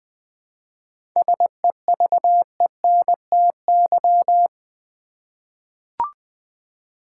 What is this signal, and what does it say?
Morse code, heard as a keyed tone, about 20 wpm, 700 Hz. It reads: SEVENTY
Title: 7 Letter Words - First 3 Letters Given - 20wpm